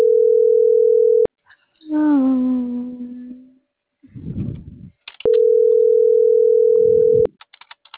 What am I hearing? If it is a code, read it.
Ringback tone (North America), 440 + 480 Hz.